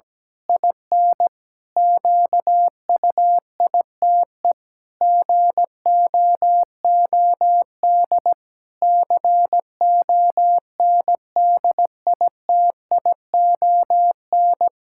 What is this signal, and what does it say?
Morse code, heard as a keyed tone, about 17 wpm, 700 Hz. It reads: IN QUITE GOOD CONDITION
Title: Pangram Challenge v2 - 17wpm